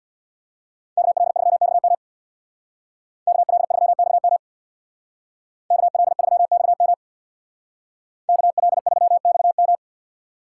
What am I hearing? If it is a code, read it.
Morse code, heard as a keyed tone, about 64 wpm, 700 Hz. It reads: KL3XM KL3XM KL3XM KL3XM